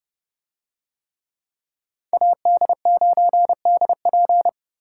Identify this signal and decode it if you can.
Morse code, heard as a keyed tone, about 29 wpm, 700 Hz. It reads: AD9DP